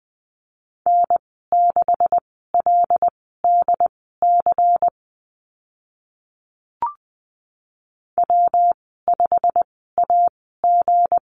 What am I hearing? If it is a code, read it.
Morse code, heard as a keyed tone, about 20 wpm, spaced as for 14 wpm, 700 Hz. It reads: N6LDC W5AG
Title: 179 Lesson 29g - Slash - Call Signs 20wpm